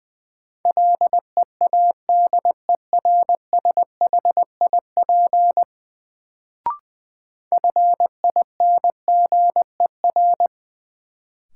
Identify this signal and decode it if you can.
Morse code, heard as a keyed tone, about 21 wpm, 700 Hz. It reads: LEADERSHIP FINGER
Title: Sentences from Top 1600 Words - New Word Drill Alt - 20wpm